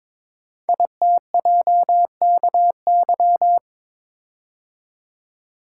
Morse code, heard as a keyed tone, about 22 wpm, 700 Hz. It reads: ITJKY
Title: Code Groups - 5 Letters - 22wpm